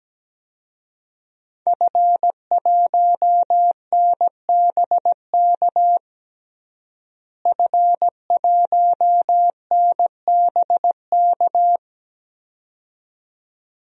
Morse code, heard as a keyed tone, about 17 wpm, 700 Hz. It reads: F1NBK F1NBK
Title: DXCC-100 Call Signs - 5 Characters - Repeated 2x - 17wpm